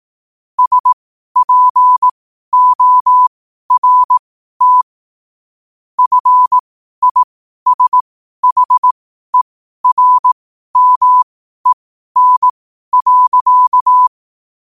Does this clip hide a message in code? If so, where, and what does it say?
Yes: Morse code from 0.6 to 14.1 s, heard as a keyed tone, about 18 wpm, spaced as for 12 wpm, 1000 Hz. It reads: SPORT FISHERMEN.